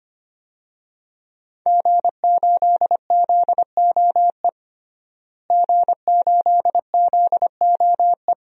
Morse code, heard as a keyed tone, about 25 wpm, 700 Hz. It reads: G8ZOE G8ZOE
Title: DXCC-100 Call Signs - 5 Characters - Repeated 2x - 25wpm